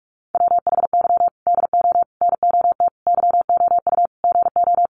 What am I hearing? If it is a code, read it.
Morse code, heard as a keyed tone, about 44 wpm, 700 Hz. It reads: WHY DO NOT YOU GO
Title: Sets of 5 Words Encore 45wpm